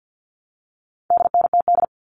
Morse code, heard as a keyed tone, about 50 wpm, 700 Hz. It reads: DNTD
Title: All Cut Numbers - 4-Digit 50wpm